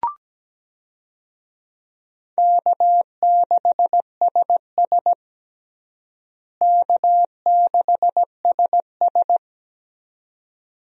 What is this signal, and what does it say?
Morse code, heard as a keyed tone, about 17 wpm, 700 Hz. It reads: K6SS K6SS
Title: USA Call Signs - 4 Characters - Repeated 2x - 17wpm